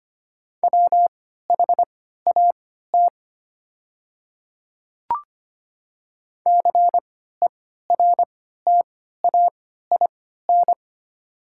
Morse code, heard as a keyed tone, about 25 wpm, spaced as for 14 wpm, 700 Hz. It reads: WHAT CERTAIN
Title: 3x Character Spacing - Top 500 Words - 25wpm